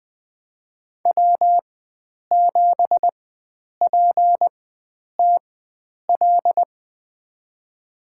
Morse code, heard as a keyed tone, about 20 wpm, spaced as for 9 wpm, 700 Hz. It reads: W7PTL
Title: USA Call Signs - 5 Characters - 4x Character Spacing - 20wpm